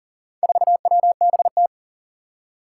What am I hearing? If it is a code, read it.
Morse code, heard as a keyed tone, about 40 wpm, 700 Hz. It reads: 4WBT
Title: Four Challenging Letter-Number Combinations 40wpm